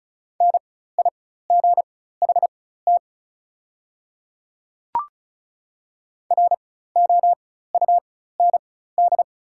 Morse code, heard as a keyed tone, about 34 wpm, spaced as for 16 wpm, 700 Hz. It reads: NIGHT ROUND
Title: 4x Character Spacing - Top 300 Words - 35wpm